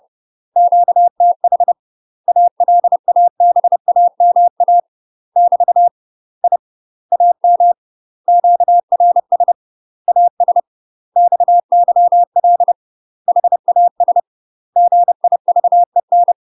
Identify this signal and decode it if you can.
Morse code, heard as a keyed tone, about 30 wpm, 700 Hz. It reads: QTH ALABAMA <BT> I AM QRS AS XYL HAS GIVEN
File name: UK MASTER QSOs TRAINING / 30WPM / UK MASTER QSOs 0051